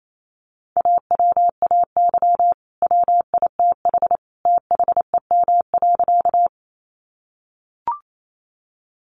Morse code, heard as a keyed tone, about 28 wpm, 700 Hz. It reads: AWAY WITH THEM.
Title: Sentences from Top 600 Words 28wpm